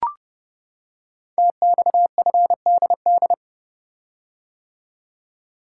Morse code, heard as a keyed tone, about 30 wpm, 700 Hz. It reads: TXFDD
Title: Code Groups - 5 Letters - 30wpm